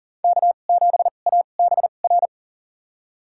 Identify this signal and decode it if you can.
Morse code, heard as a keyed tone, about 40 wpm, spaced as for 29 wpm, 700 Hz. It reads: K7ABR
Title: USA Call Signs - 5 Characters - 2x Character Spacing - 40wpm